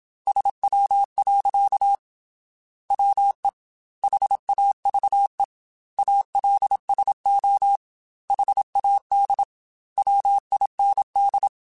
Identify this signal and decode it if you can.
Morse code, heard as a keyed tone, about 27 wpm, 800 Hz. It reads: SW. WE HAVE ALSO HAD WIND
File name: Weather Sets / Weather Sets-mp3s / Weather 5 5 @25